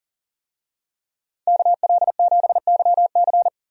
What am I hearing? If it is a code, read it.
Morse code, heard as a keyed tone, about 40 wpm, 700 Hz. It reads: KL7YC
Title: DXCC-100 Call Signs - 5 Characters - 40wpm